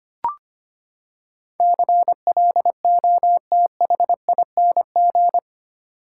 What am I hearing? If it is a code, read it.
Morse code, heard as a keyed tone, about 25 wpm, 700 Hz. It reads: CLOTHING
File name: Eight Letter Words - 25wpm